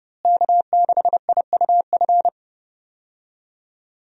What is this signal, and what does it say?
Morse code, heard as a keyed tone, about 32 wpm, 700 Hz. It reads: K6IUF
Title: USA Call Signs - 5 Characters - Repeated 2x - 30wpm